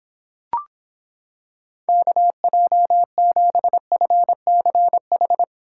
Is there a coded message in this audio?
Yes: Morse code from 1.9 to 5.4 s, heard as a keyed tone, about 26 wpm, 700 Hz. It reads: KJ7FCH